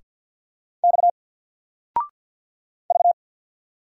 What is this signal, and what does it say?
Morse code, heard as a keyed tone, about 48 wpm, 700 Hz. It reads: X V
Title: Single Character (Letter:Number:Punctuation:Prosign) 50wpm